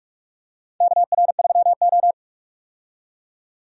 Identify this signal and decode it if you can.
Morse code, heard as a keyed tone, about 46 wpm, 700 Hz. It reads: KR3O
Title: USA Call Signs - 4 Characters - 45wpm